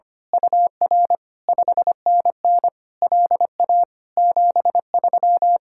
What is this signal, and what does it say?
Morse code, heard as a keyed tone, about 24 wpm, 700 Hz. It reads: UR 5NN LA 73
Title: Speed-Racing Common QSO Phrases Encore - 25wpm